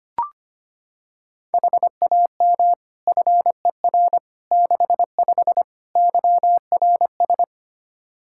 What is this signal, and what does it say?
Morse code, heard as a keyed tone, about 25 wpm, 700 Hz. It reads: HAM FER 65 YRS
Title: Practice QSO - Exchange 3 - More Info - 25wpm